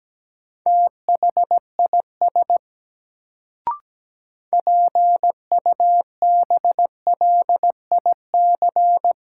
Morse code, heard as a keyed tone, about 17 wpm, 700 Hz. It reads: THIS PUBLIC
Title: Sets of 2 Words - New Word Drill - 17wpm